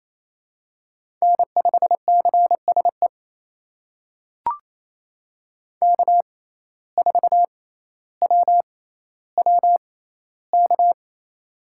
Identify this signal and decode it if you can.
Morse code, heard as a keyed tone, about 28 wpm, spaced as for 14 wpm, 700 Hz. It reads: N5CSE K4WWK
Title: USA Call Signs - 5 Characters - 6x Character Spacing - 28wpm